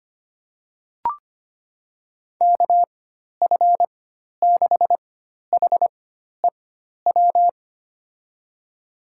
Morse code, heard as a keyed tone, about 25 wpm, spaced as for 12 wpm, 700 Hz. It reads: KF6HEW
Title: USA Call Signs - 6 Characters - 4x Character Spacing - 25wpm